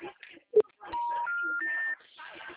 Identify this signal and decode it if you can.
Special information tone, with speech.
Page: Americana 2